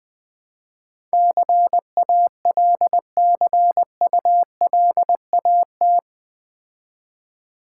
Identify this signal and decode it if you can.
Morse code, heard as a keyed tone, about 20 wpm, 700 Hz. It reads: CALCULAT
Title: Build A Word - 4 Letters At A Time - 11-Letter Words - 20wpm